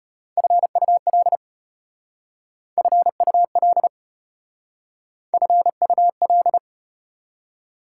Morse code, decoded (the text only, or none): FUL FUL FUL